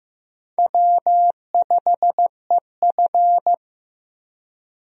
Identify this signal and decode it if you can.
Morse code, heard as a keyed tone, about 15 wpm, 700 Hz. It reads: W5EF
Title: USA Call Signs - 4 Characters - 15wpm